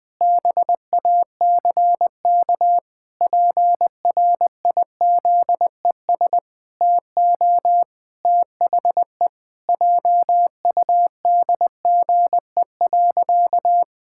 Morse code, heard as a keyed tone, about 20 wpm, 700 Hz. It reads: BACK PRIZES TO THE JUDGE.
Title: Pangram Challenge v2 - 20wpm